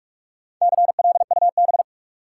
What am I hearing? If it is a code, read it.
Morse code, heard as a keyed tone, about 46 wpm, 700 Hz. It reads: CLUB
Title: Sentences from Top 1200 Words - New Word Drill Alt - 45wpm